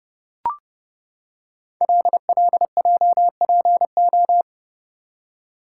Morse code, heard as a keyed tone, about 28 wpm, 700 Hz. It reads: LLJPO
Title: Code Groups - 5 Letters - 30wpm